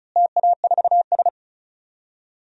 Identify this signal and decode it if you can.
Morse code, heard as a keyed tone, about 34 wpm, 700 Hz. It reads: TA4S
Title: DXCC-100 Call Signs - 4 Characters - Repeated 3x - 35wpm